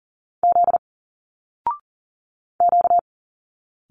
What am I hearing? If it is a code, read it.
Morse code, heard as a keyed tone, about 40 wpm, 700 Hz. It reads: Z Q
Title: Single Letters 40wpm